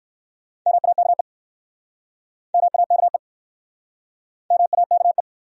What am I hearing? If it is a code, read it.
Morse code, heard as a keyed tone, about 63 wpm, 700 Hz. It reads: MAKE MAKE MAKE